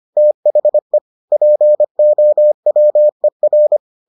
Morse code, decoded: THE POWER

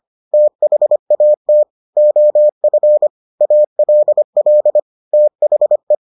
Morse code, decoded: THAT OF ALL THE

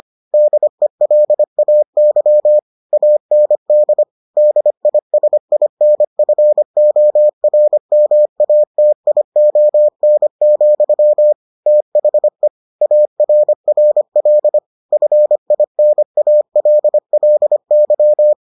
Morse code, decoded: DELAY AND DISINFORMATION, THE ARRL FINALLY

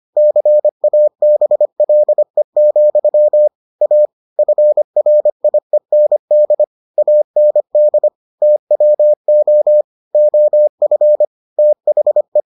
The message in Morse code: CABLE, A FRIEND AND TWO OF THE

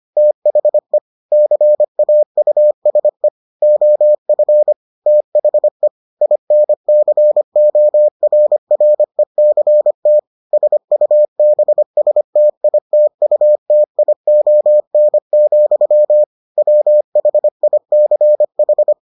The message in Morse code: THE CAUSE OF THE INCORRECT SUBSTITUTION, WHICH

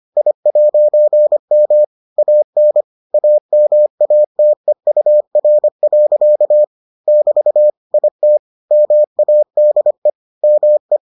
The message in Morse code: I'M AN AMATEUR. <BT> IT MADE ME

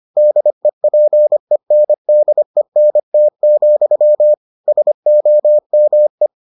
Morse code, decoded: DEPENDENT, SOME

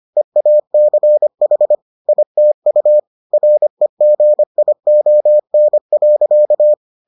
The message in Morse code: EACH ITU REGION.

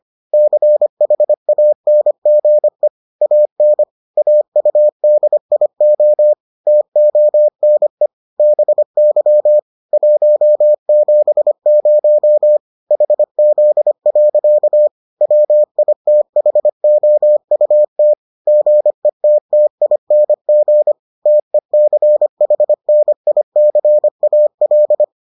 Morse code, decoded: CHANGE AN AUDIO TONE BY 170 HZ. WITHOUT GETTING TECHNICAL

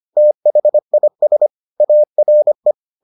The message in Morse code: THIS ARE

